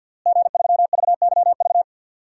Morse code, heard as a keyed tone, about 52 wpm, 700 Hz. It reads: G3VYV
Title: DXCC-100 Call Signs - 5 Characters - Repeated 3x - 50wpm